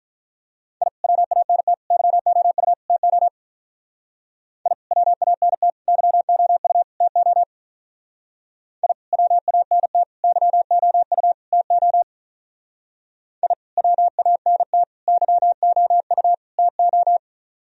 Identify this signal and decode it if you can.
Morse code, heard as a keyed tone, about 42 wpm, 700 Hz. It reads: I WANT YOU TO I WANT YOU TO I WANT YOU TO I WANT YOU TO
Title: Speed-Racing - Sets of 4 Words 35wpm